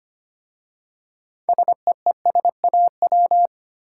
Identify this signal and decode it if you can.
Morse code, heard as a keyed tone, about 25 wpm, 700 Hz. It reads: SEESAW
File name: practice / 25wpm / Easy Words with Hints - 25wpm